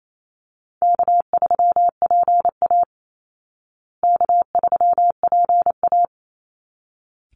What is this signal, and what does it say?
Morse code, heard as a keyed tone, about 28 wpm, 700 Hz. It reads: K3PA K3PA
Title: Easy Call Signs Repeated 2x - 28wpm